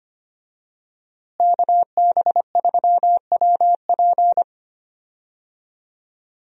Morse code, heard as a keyed tone, about 25 wpm, 700 Hz. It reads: KB3WP
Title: USA Call Signs - 5 Characters - Repeated 2x - 25wpm